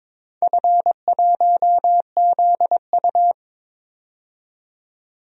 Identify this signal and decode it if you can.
Morse code, heard as a keyed tone, about 22 wpm, 700 Hz. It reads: F1ZU
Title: DXCC-100 Call Signs - 4 Characters - 22wpm